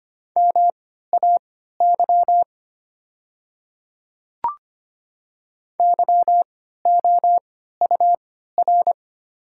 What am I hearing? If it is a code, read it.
Morse code, heard as a keyed tone, about 25 wpm, spaced as for 14 wpm, 690 Hz. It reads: MAY YOUR
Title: 3x Character Spacing - Top 500 Words - 25wpm